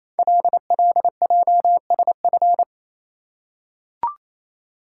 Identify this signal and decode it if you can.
Morse code, heard as a keyed tone, about 27 wpm, 700 Hz. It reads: LLJSF
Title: Code Groups - 5 Letters - 28wpm